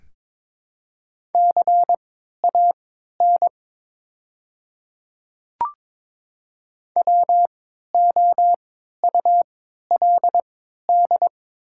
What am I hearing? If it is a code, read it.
Morse code, heard as a keyed tone, about 22 wpm, spaced as for 12 wpm, 700 Hz. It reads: CAN WOULD
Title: 3x Character Spacing - Top 100 Words - 22wpm